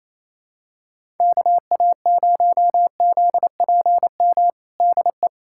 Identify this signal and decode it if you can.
Morse code, heard as a keyed tone, about 28 wpm, 700 Hz. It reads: KA0ZPM DE
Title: USA Callsign DE Callsign - 6 Characters - 28wpm